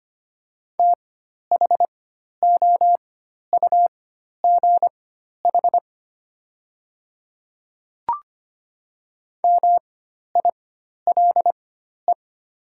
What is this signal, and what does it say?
Morse code, heard as a keyed tone, about 25 wpm, spaced as for 12 wpm, 700 Hz. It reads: THOUGH MILE
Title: 4x Character Spacing - Top 500 Words - 25wpm